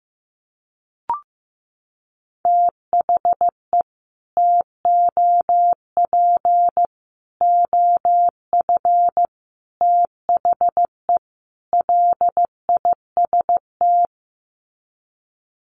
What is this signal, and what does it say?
Morse code, heard as a keyed tone, about 15 wpm, 700 Hz. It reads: THE TOP OF THE LIST
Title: Sets of 5 Words Encore2 15wpm